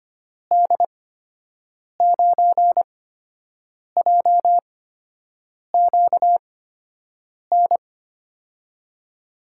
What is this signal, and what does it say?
Morse code, heard as a keyed tone, about 25 wpm, spaced as for 7 wpm, 700 Hz. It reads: D9JQN
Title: USA Call Signs - 8x Character Spacing - 25wpm